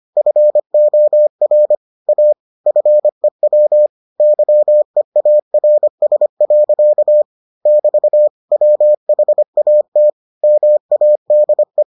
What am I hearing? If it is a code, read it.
Morse code, heard as a keyed tone, about 25 wpm, 600 Hz. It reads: FOR A FEW YEARS. <BT> WHAT MADE